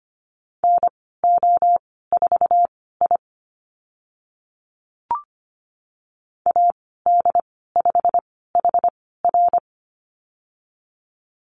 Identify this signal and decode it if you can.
Morse code, heard as a keyed tone, about 25 wpm, spaced as for 16 wpm, 700 Hz. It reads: NO4I AD5HR
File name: Lesson 15g - Number 5 - Call Signs 25wpm